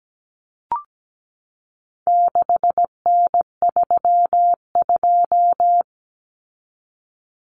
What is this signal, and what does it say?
Morse code, heard as a keyed tone, about 17 wpm, 700 Hz. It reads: 6N32
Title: Common Cut Numbers - 4-Digit 17wpm